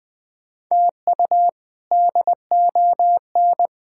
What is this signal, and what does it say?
Morse code, heard as a keyed tone, about 20 wpm, 700 Hz. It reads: TU DON
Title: K1USN SST QSO Practice - 20wpm